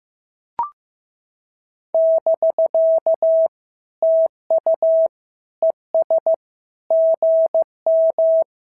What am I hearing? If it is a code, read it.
Morse code, heard as a keyed tone, about 14 wpm, 650 Hz. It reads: <BK> TU ES GM
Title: SOTA QSO Practice 15wpm